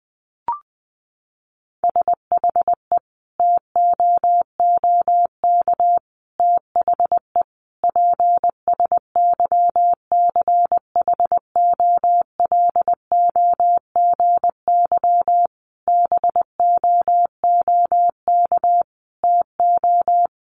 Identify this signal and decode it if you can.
Morse code, heard as a keyed tone, about 20 wpm, 700 Hz. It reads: SHE TOOK THE PSYCHOLOGY BOOK TO
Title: Sentences from Top 1400 Words 20wpm